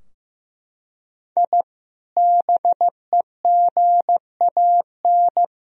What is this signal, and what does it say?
Morse code, heard as a keyed tone, about 16 wpm, 700 Hz. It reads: I BEGAN
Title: Sentences from Top 1000 Words - Review Part 2 - 15wpm